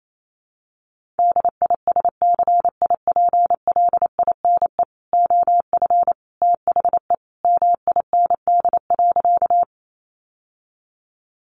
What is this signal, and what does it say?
Morse code, heard as a keyed tone, about 28 wpm, 700 Hz. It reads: DISCIPLINE OF THE MIND.
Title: Sentences from Top 900 Words 28wpm